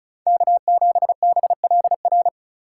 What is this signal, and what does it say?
Morse code, heard as a keyed tone, about 36 wpm, 700 Hz. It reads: K7BLR
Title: USA Call Signs - 5 Characters - 2x Character Spacing - 35wpm